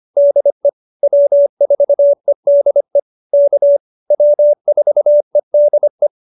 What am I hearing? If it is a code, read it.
Morse code, heard as a keyed tone, about 25 wpm, 590 Hz. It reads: DE W4EDE K W4EDE